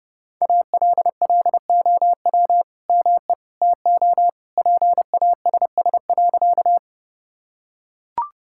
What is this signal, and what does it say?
Morse code, heard as a keyed tone, about 30 wpm, 700 Hz. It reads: ALLOW ME TO PASS.
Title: Sentences from Top 1000 Words - Review Part 5 - 30wpm